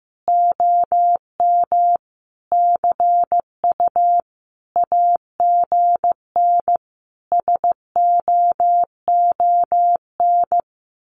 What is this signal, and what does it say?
Morse code, heard as a keyed tone, about 15 wpm, 700 Hz. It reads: OM CU AGN SOON
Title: Common QSO Phrases 15wpm